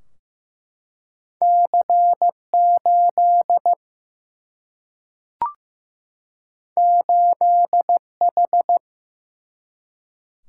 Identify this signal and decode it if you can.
Morse code, heard as a keyed tone, about 15 wpm, 700 Hz. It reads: C8 8H
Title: Single-Letter Single-Number 15wpm